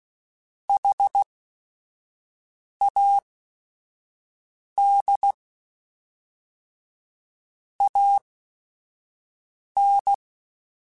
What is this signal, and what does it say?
Morse code, heard as a keyed tone, about 16 wpm, spaced as for 5 wpm, 800 Hz. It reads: HAD AN
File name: Technical Subjects / Technical Subjects-mp3s / AND Gate @5